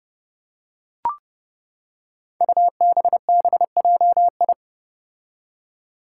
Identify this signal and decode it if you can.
Morse code, heard as a keyed tone, about 32 wpm, 700 Hz. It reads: UBBJI
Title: Five Challenging Letter-Number Combinations 30wpm